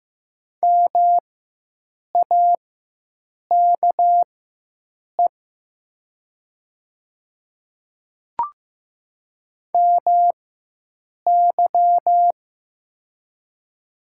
Morse code, heard as a keyed tone, about 15 wpm, spaced as for 7 wpm, 700 Hz. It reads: MAKE MY
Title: 4x Character Spacing - Top 200 Words - 15wpm